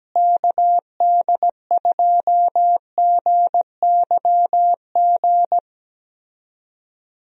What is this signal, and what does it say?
Morse code, heard as a keyed tone, about 17 wpm, 700 Hz. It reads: KD2GYG
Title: USA Call Signs - 6 Characters - Repeated 2x - 17wpm